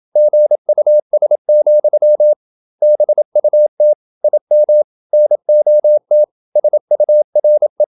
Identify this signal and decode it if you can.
Morse code, heard as a keyed tone, about 27 wpm, 600 Hz. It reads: GUS, BUT IM NOT SURE
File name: Never 0027wpm